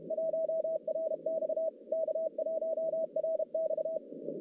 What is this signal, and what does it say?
Morse code, heard as a keyed tone, about 32 wpm, 640 Hz. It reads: 1RX K1RX
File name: contests / CW-SS-k5tr- -1200